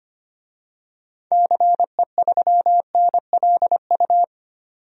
Morse code, heard as a keyed tone, about 25 wpm, 700 Hz. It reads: CE3NLU
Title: DXCC-100 Call Signs - 6 Characters - 25wpm